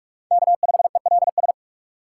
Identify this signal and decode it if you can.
Morse code, heard as a keyed tone, about 42 wpm, 700 Hz. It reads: K5ELS
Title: USA Call Signs - 5 Characters - 3x Character Spacing - 45wpm